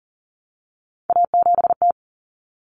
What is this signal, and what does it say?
Morse code, heard as a keyed tone, about 40 wpm, 700 Hz. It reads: A7T
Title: Common Cut Numbers - 3-Digit 40wpm